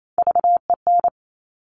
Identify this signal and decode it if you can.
Morse code, heard as a keyed tone, about 28 wpm, 700 Hz. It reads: VEN
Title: Top Three Letter Combinations in Words 28wpm